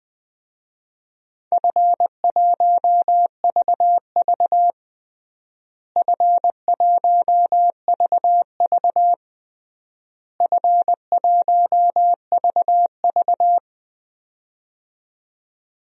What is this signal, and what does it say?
Morse code, heard as a keyed tone, about 20 wpm, 700 Hz. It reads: F1VV F1VV F1VV
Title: DXCC-100 Call Signs - 4 Characters - Repeated 3x - 20wpm